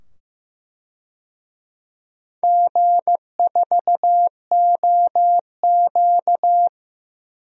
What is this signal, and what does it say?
Morse code, heard as a keyed tone, about 15 wpm, 700 Hz. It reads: G4OQ
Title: DXCC-100 Call Signs - 4 Characters - 15wpm